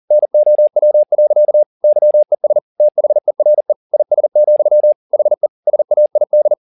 Morse code, heard as a keyed tone, about 40 wpm, 600 Hz. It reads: NOW. YES THERE IS, HE SAID
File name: the old man and the sea 2 40wpm